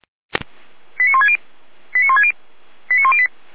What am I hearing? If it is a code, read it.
Five-tone selective call (ZVEI), heard as quick runs of five tones, tones of about 70 ms. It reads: 89160 89160 89108